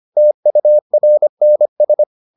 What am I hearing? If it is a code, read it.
Morse code, heard as a keyed tone, about 25 wpm, 600 Hz. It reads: TURNS